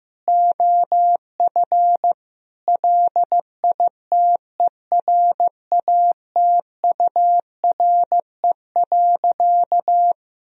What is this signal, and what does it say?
Morse code, heard as a keyed tone, about 15 wpm, 700 Hz. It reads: OF LITERATURE.
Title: Sentences from Top 1000 Words - Review Part 3 - 15wpm